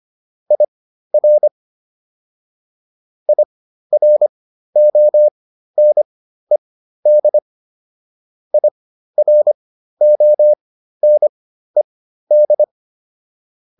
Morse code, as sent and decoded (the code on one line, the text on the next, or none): .. .-. / .. .-. --- -. -.. / .. .-. --- -. -..
IR IRONED IRONED